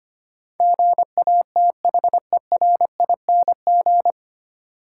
Morse code, heard as a keyed tone, about 25 wpm, 700 Hz. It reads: GATHERING
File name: practice / 25wpm / 9 Letter Words - First 3 Letters Given - 25wpm